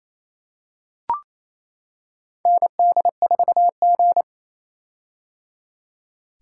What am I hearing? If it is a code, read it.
Morse code, heard as a keyed tone, about 27 wpm, 700 Hz. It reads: ND4G
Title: K1USN SST QSO Practice - 28wpm